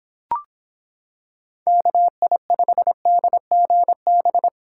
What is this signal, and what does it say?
Morse code, heard as a keyed tone, about 26 wpm, 700 Hz. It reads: KI5DGB